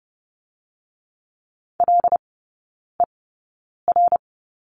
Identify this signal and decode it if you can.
Morse code, heard as a keyed tone, about 30 wpm, 700 Hz. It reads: L E R
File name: Lesson 19e - Letter w - Three Characters 30wpm